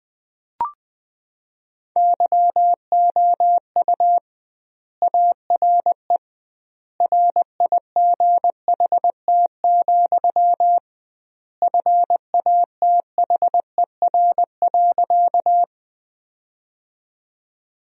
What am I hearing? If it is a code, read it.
Morse code, heard as a keyed tone, about 20 wpm, 700 Hz. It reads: YOU ARE RIGHT, FATHER.
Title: 2x Word-Spacing - Sentences from Top 300 Words - 20wpm